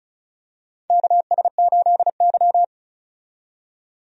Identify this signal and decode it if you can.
Morse code, heard as a keyed tone, about 34 wpm, 700 Hz. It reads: KS8Y
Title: USA Call Signs - 4 Characters - 35wpm